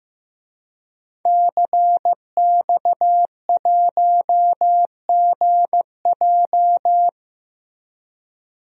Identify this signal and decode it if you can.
Morse code, heard as a keyed tone, about 15 wpm, 700 Hz. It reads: CX1GJ
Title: DXCC-100 Call Signs - 5 Characters - 15wpm